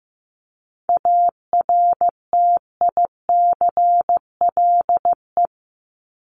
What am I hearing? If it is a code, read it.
Morse code, heard as a keyed tone, about 15 wpm, 700 Hz. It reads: ARTICLE
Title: Top 751-1000 Words 15wpm